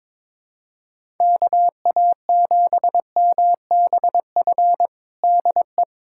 Morse code, heard as a keyed tone, about 22 wpm, 700 Hz. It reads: KA7MBF DE
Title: USA Callsign DE Callsign - 6 Characters - 22wpm